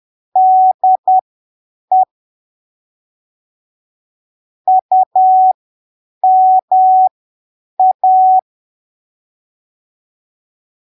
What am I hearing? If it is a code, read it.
Morse code, heard as a keyed tone, about 10 wpm, spaced as for 7 wpm, 750 Hz. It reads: DE UMA
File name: arquivos / texto / texto 27 10ppm